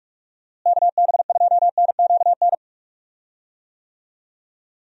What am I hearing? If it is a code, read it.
Morse code, heard as a keyed tone, about 46 wpm, 700 Hz. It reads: KB2NQN
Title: USA Call Signs - 6 Characters - Repeated 2x - 45wpm